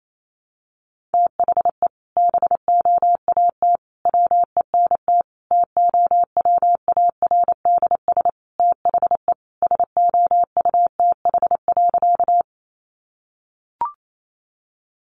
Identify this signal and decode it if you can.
Morse code, heard as a keyed tone, about 28 wpm, 700 Hz. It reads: THE BOAT WENT TOWARDS THE SOUTH.
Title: Sentences from Top 1100 Words 28wpm